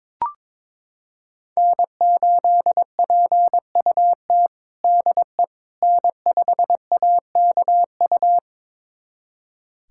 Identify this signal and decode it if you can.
Morse code, heard as a keyed tone, about 23 wpm, 700 Hz. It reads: N8PUT DE N5AKU
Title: USA Callsign DE Callsign - 5 Characters - 22wpm